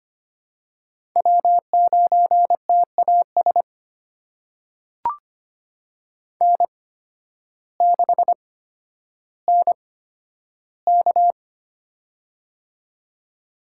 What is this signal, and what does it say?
Morse code, heard as a keyed tone, about 25 wpm, spaced as for 12 wpm, 700 Hz. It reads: W9TAS N6NK
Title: USA Call Signs - 8x Character Spacing - 25wpm